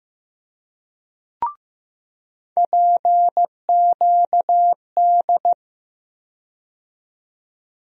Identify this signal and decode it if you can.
Morse code, heard as a keyed tone, about 15 wpm, 700 Hz. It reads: PQD
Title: USA Call Sign Postfixes - 3 Letters 15wpm